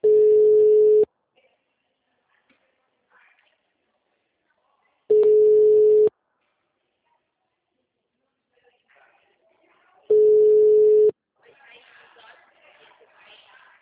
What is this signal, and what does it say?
Ringback tone (Europe, 425 Hz), with speech.
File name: out- -2005- - - .137